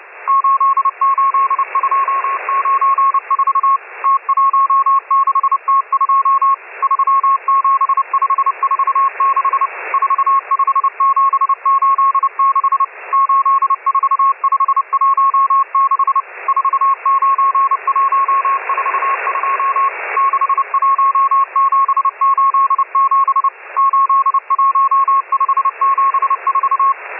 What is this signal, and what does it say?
Morse code, heard as a keyed tone, about 30 wpm, 1090 Hz, with radio noise. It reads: 88294 T16T2 37546 45786 84516 49142 61787 81574